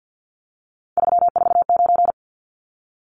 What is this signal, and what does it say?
Morse code, heard as a keyed tone, about 50 wpm, 700 Hz. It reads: 349
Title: Numbers - 3-Digit 50wpm